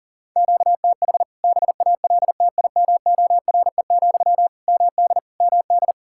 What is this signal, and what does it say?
Morse code, heard as a keyed tone, about 41 wpm, 700 Hz. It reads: QTH BALTIMORE, MD MD
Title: K1USN SST QSO Practice 40wpm